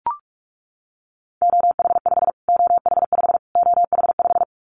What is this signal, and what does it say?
Morse code, heard as a keyed tone, about 44 wpm, 700 Hz. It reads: OH5 OH5 OH5
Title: Callsign Prefixes for Top 25 DXCC - Repeated 3x - 45wpm